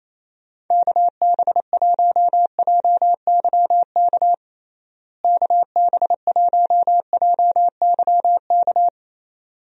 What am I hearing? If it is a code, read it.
Morse code, heard as a keyed tone, about 27 wpm, 700 Hz. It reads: KB1JYK KB1JYK
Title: USA Call Signs - 6 Characters - Repeated 2x - 28wpm